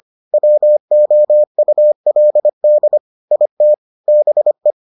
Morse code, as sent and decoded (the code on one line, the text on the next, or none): .-- --- ..- .-.. -.. / .. - / -... .
WOULD IT BE